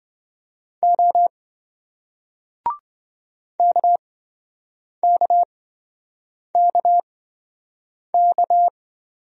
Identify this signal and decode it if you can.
Morse code, heard as a keyed tone, about 27 wpm, spaced as for 7 wpm, 700 Hz. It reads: O KKKK